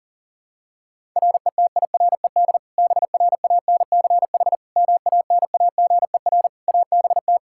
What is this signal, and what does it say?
Morse code, heard as a keyed tone, about 37 wpm, 700 Hz. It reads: RETIRED BRANCH MANAGER ABT
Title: Practice QSO - Exchange 3 - More Info - 40wpm